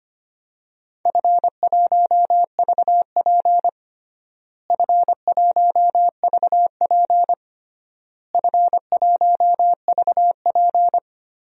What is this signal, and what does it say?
Morse code, heard as a keyed tone, about 25 wpm, 700 Hz. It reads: F1VP F1VP F1VP
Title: DXCC-100 Call Signs - 4 Characters - Repeated 3x - 25wpm